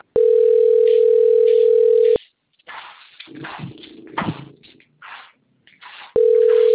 Ringback tone (North America), 440 + 480 Hz.